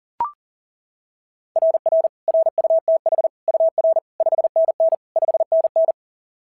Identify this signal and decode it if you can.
Morse code, heard as a keyed tone, about 39 wpm, 660 Hz. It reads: RR RUTH UR 5NN 5NN
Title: SOTA QSO Practice 40wpm